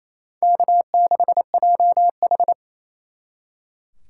Morse code, heard as a keyed tone, about 36 wpm, 700 Hz. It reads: K6JH